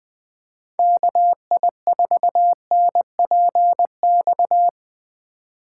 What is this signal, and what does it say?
Morse code, heard as a keyed tone, about 22 wpm, 700 Hz. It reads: KI4NPX
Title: Speed Racing - USA 2x3 Call Signs 20wpm